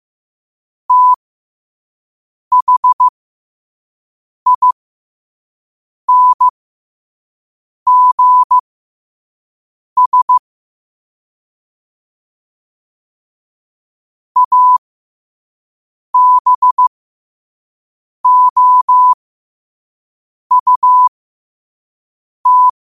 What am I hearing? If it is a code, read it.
Morse code, heard as a keyed tone, about 15 wpm, spaced as for 5 wpm, 1000 Hz. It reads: THINGS ABOUT
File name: qotdfiles / 5wpm